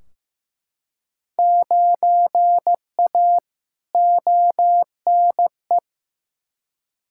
Morse code, heard as a keyed tone, about 15 wpm, 700 Hz. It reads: 9A ONE
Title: Field Day Exchanges Encore - 15wpm